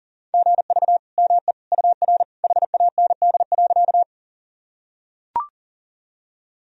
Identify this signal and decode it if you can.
Morse code, heard as a keyed tone, about 37 wpm, 700 Hz. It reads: GV ME UR HAND.
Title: Sentences from Top 200 Words and Abbreviations 40wpm